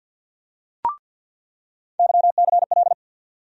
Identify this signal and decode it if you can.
Morse code, heard as a keyed tone, about 47 wpm, 700 Hz. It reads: YCL
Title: USA Call Sign Postfixes - 3 Letters 50wpm